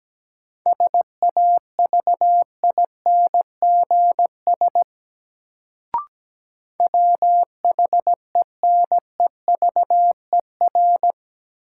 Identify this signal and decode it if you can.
Morse code, heard as a keyed tone, about 17 wpm, 700 Hz. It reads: SAVINGS WHENEVER
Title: Sentences from Top 1200 Words - New Word Drill - 17wpm